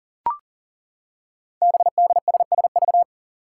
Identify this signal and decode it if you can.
Morse code, heard as a keyed tone, about 42 wpm, 700 Hz. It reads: BDSSV
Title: Five Challenging Letter-Number Combinations 40wpm